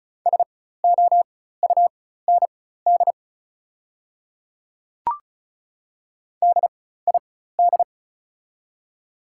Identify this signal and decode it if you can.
Morse code, heard as a keyed tone, about 34 wpm, spaced as for 16 wpm, 700 Hz. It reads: SOUND DID
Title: 4x Character Spacing - Top 100 Words - 35wpm